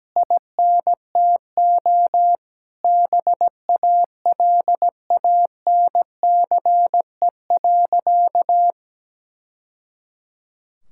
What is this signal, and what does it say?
Morse code, heard as a keyed tone, about 17 wpm, 700 Hz. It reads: INTO BALANCE.
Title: Sentences from Top 1000 Words - Review Part 6 - 17wpm